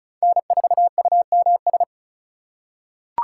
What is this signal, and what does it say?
Morse code, heard as a keyed tone, about 34 wpm, 700 Hz. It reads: N4UMS
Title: USA Call Signs - 5 Characters - 8x Character Spacing - 35wpm